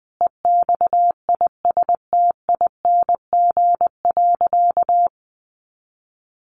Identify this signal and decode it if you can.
Morse code, heard as a keyed tone, about 19 wpm, 700 Hz. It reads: EXISTING.
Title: Sentences from Top 1500 Words 20wpm